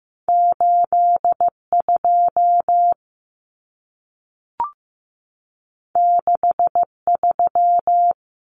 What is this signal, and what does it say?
Morse code, heard as a keyed tone, about 15 wpm, 700 Hz. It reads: 8 2 6 3
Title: Common Cut Numbers - 2-Digit 15wpm